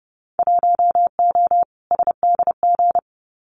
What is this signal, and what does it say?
Morse code, heard as a keyed tone, about 30 wpm, 700 Hz. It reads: 1O SDG
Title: Winter Field Day Exchanges Encore 30wpm